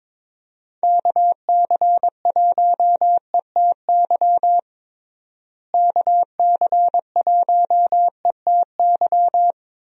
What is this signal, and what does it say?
Morse code, heard as a keyed tone, about 22 wpm, 700 Hz. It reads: KC1ETY KC1ETY
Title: USA Call Signs - 6 Characters - Repeated 3x - 22wpm